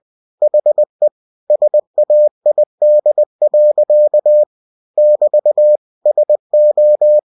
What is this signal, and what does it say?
Morse code, heard as a keyed tone, about 20 wpm, 600 Hz. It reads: HE SAID. <BT> SO
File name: cw-books / winnie-the-pooh / winnie-20wpm-page-0017